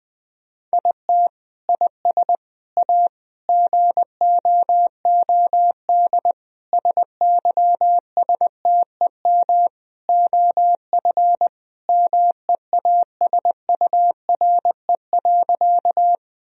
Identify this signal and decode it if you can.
Morse code, heard as a keyed tone, about 20 wpm, 700 Hz. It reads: IT IS A GOOD SYSTEM OF MEASURE.
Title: Sentences from Top 1000 Words - Review Part 3 - 20wpm